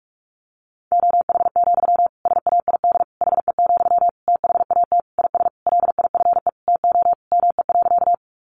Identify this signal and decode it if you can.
Morse code, heard as a keyed tone, about 44 wpm, 700 Hz. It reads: OH, SAID HE, THAT IS LIFE TO ME.
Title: Sentences from Top 500 Words 45wpm